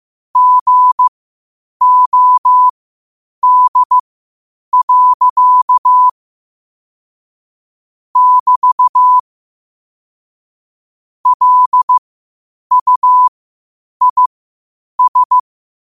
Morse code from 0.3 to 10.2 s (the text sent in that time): GOD. <BT>